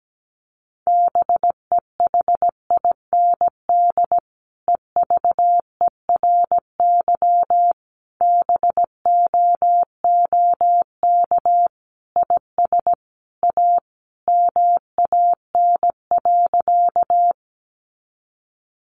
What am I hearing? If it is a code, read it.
Morse code, heard as a keyed tone, about 17 wpm, 700 Hz. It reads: BEHIND EVERY BOOK IS A MAN.
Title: Sentences from Top 500 Words 17wpm